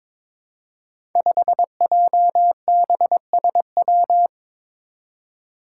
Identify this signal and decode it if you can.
Morse code, heard as a keyed tone, about 21 wpm, 700 Hz. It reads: 5JBSW
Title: Five Challenging Letter-Number Combinations 22wpm